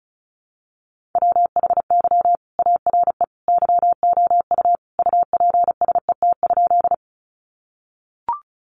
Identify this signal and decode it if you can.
Morse code, heard as a keyed tone, about 34 wpm, 700 Hz. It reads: WHY ARE YOU UPSET?
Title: Sentences from Top 800 Words 35wpm